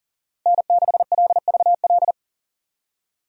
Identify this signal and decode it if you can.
Morse code, heard as a keyed tone, about 40 wpm, 700 Hz. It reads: N6LVL
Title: USA Call Signs - 5 Characters - Repeated 2x - 40wpm